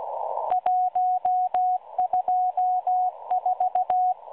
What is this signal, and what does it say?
Morse code, heard as a keyed tone, about 16 wpm, 730 Hz. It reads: 124
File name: M12 - 8116 kHz CW 2-11-14